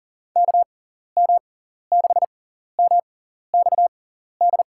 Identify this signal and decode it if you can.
Morse code, heard as a keyed tone, about 37 wpm, spaced as for 14 wpm, 700 Hz. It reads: KM6MXD
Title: USA Call Signs - 6 Characters - 6x Character Spacing - 40wpm